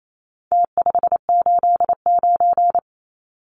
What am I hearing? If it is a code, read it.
Morse code, heard as a keyed tone, about 28 wpm, 700 Hz. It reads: T589
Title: Numbers - Contest Style - 4-Digit 28wpm